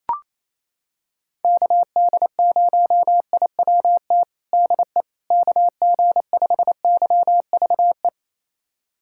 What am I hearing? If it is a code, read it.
Morse code, heard as a keyed tone, about 28 wpm, 700 Hz. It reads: KD0IWT DE KG5YVE
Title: USA Callsign DE Callsign - 6 Characters - 28wpm